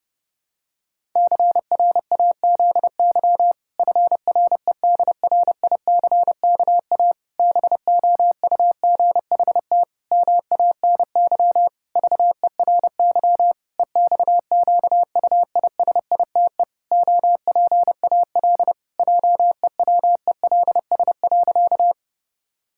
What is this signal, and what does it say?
Morse code, heard as a keyed tone, about 30 wpm, 700 Hz. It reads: CRAZY FREDRICKA BOUGHT MANY VERY EXQUISITE OPAL JEWELS.
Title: Pangram Challenge v2 - 30wpm